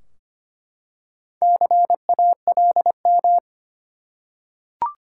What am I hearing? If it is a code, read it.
Morse code, heard as a keyed tone, about 25 wpm, 700 Hz. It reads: CALM
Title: Four Letter Words - 25wpm